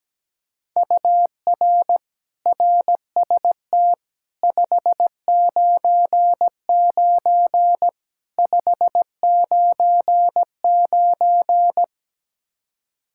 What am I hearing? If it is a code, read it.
Morse code, heard as a keyed tone, about 17 wpm, 700 Hz. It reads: UR RST 599 599
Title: Practice QSO - Exchange 1 - Beginning - 17wpm